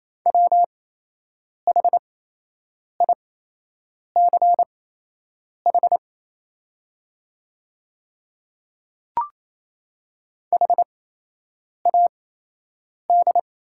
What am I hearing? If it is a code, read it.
Morse code, heard as a keyed tone, about 28 wpm, spaced as for 8 wpm, 700 Hz. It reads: WHICH HAD
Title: 8x Character Spacing - Top 100 Words - 28wpm